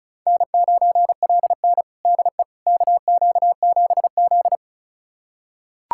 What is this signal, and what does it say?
Morse code, heard as a keyed tone, about 34 wpm, 700 Hz. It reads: N9LN DE KQ7Z
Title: USA Callsign DE Callsign - 4 Characters - 35wpm